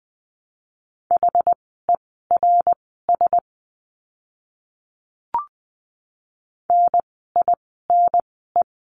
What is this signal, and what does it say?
Morse code, heard as a keyed tone, about 20 wpm, spaced as for 14 wpm, 700 Hz. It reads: HERS NINE
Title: 061 Lesson 11f - Letter h - Words 20wpm